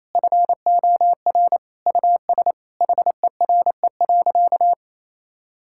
Morse code, heard as a keyed tone, about 28 wpm, 700 Hz. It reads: FOR US HERE.
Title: Sentences from Top 1000 Words - Review Part 5 - 28wpm